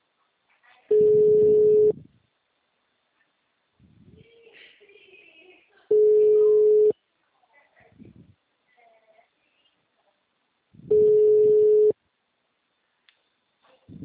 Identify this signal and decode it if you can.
Ringback tone (Europe, 425 Hz).